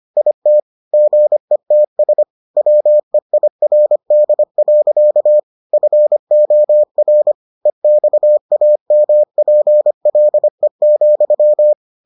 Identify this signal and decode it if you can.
Morse code, heard as a keyed tone, about 25 wpm, 600 Hz. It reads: IT GETS WEIRD. FOR EXAMPLE,